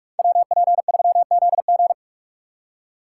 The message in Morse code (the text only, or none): WP3ZG